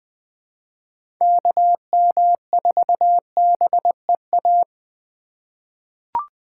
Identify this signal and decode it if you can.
Morse code, heard as a keyed tone, about 20 wpm, 700 Hz. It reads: KM4BEA
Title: USA Call Signs - 2x Character Spacing - 20wpm